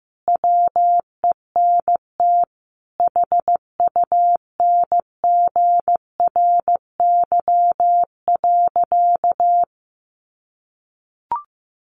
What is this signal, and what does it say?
Morse code, heard as a keyed tone, about 15 wpm, 700 Hz. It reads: WENT HUNGRY.
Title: Sentences from Top 1400 Words 15wpm